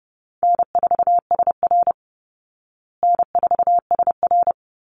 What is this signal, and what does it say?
Morse code, heard as a keyed tone, about 28 wpm, 700 Hz. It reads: N4SR N4SR
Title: Easy Call Signs Repeated 2x - 30wpm